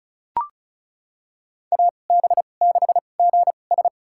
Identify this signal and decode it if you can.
Morse code, heard as a keyed tone, about 34 wpm, spaced as for 25 wpm, 690 Hz. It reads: AB6GS
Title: USA Call Signs - 5 Characters - 2x Character Spacing - 35wpm